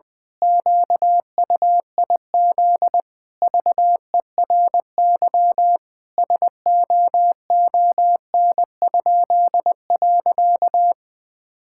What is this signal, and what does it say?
Morse code, heard as a keyed tone, about 21 wpm, 700 Hz. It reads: QUIZ VERY SOON?.
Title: Pangram Challenge v2 - 20wpm